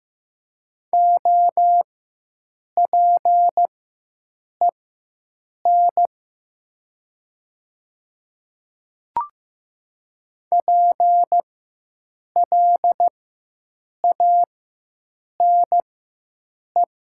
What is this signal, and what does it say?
Morse code, heard as a keyed tone, about 15 wpm, spaced as for 7 wpm, 700 Hz. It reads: OPEN PLANE
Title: 4x Character Spacing - Top 500 Words - 15wpm